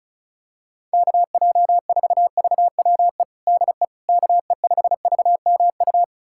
Morse code, heard as a keyed tone, about 36 wpm, 700 Hz. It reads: KJ4VWE DE KE5VMU
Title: USA Callsign DE Callsign - 6 Characters - 35wpm